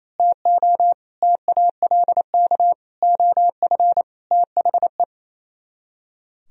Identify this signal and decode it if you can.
Morse code, heard as a keyed tone, about 27 wpm, 700 Hz. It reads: TO TALK OF THE
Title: Speed-Racing - Sets of 5 Words 28wpm